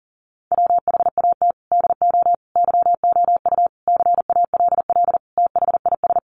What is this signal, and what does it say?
Morse code, heard as a keyed tone, about 40 wpm, 700 Hz. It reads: WHAT DO YOU CALL THIS
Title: Sentences from Top 1400 Words 40wpm